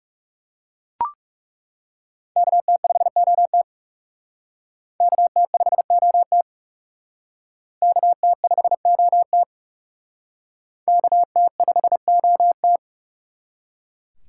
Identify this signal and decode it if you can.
Morse code, heard as a keyed tone, about 37 wpm, 700 Hz. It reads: KT5OT KT5OT KT5OT KT5OT
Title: Speed Racing - USA 2x2 Call Signs 30wpm